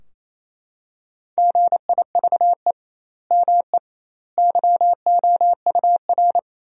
Morse code, heard as a keyed tone, about 28 wpm, 700 Hz. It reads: GIVE ME YOUR
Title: 2x Word-Spacing - Sentences from Top 200 Words - 28wpm